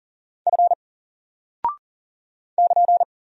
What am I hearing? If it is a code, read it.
Morse code, heard as a keyed tone, about 40 wpm, 700 Hz. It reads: F <KN>
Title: Single Character (Letter:Number:Punctuation:Prosign) Alt 40wpm